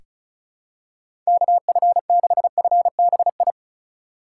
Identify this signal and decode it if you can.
Morse code, heard as a keyed tone, about 36 wpm, 700 Hz. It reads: KF6FBI
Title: USA Call Signs - 6 Characters - 4x Character Spacing - 35wpm